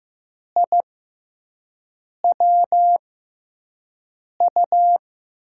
Morse code, heard as a keyed tone, about 15 wpm, spaced as for 5 wpm, 700 Hz. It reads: IWU
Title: USA Call Signs - 6x Character Spacing - 15wpm